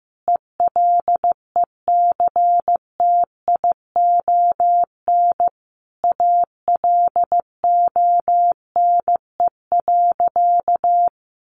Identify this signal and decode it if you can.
Morse code, heard as a keyed tone, about 15 wpm, 700 Hz. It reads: ELECTION ALONE.
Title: Sentences from Top 1500 Words 15wpm